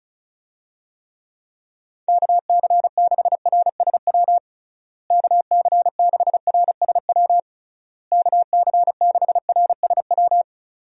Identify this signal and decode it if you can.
Morse code, heard as a keyed tone, about 34 wpm, 700 Hz. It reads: KC6RSW KC6RSW KC6RSW
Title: USA Call Signs - 6 Characters - Repeated 3x - 35wpm